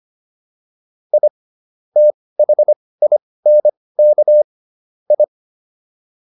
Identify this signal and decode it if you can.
Morse code, heard as a keyed tone, about 25 wpm, spaced as for 18 wpm, 600 Hz. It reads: I THINK I